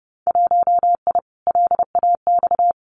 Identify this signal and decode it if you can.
Morse code, heard as a keyed tone, about 30 wpm, 700 Hz. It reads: 1I LAX
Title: Winter Field Day Exchanges Encore 30wpm